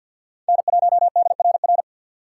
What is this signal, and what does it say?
Morse code, heard as a keyed tone, about 50 wpm, 700 Hz. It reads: N1DRR
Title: USA Call Signs - 6x Character Spacing - 50wpm